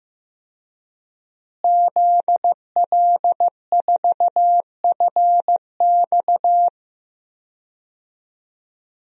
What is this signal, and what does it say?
Morse code, heard as a keyed tone, about 16 wpm, 700 Hz. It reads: ZL4FX
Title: DXCC-100 Call Signs - 5 Characters - Repeated 3x - 15wpm